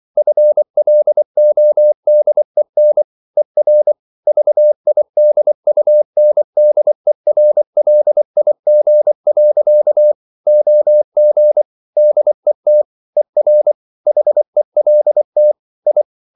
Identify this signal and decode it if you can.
Morse code, heard as a keyed tone, about 24 wpm, 600 Hz. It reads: FLODEN ER VIDUNDERLIG. OG DET ER HELT I
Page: Ferie 24 wpm